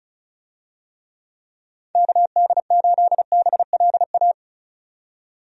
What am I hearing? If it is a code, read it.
Morse code, heard as a keyed tone, about 36 wpm, 700 Hz. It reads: KD8BLA
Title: USA Call Signs - 6 Characters - 35wpm